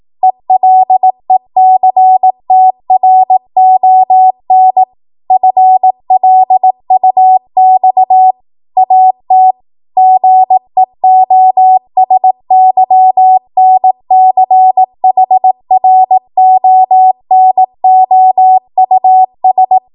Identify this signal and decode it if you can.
Morse code, heard as a keyed tone, about 18 wpm, 750 Hz. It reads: ELECTRON FLUX AT GEOSYNCHRONOUS
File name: Morse / 18WPM